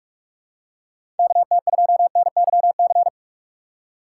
Morse code, heard as a keyed tone, about 43 wpm, 700 Hz. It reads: KT2NYC
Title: USA Call Signs - 6 Characters - 2x Character Spacing - 45wpm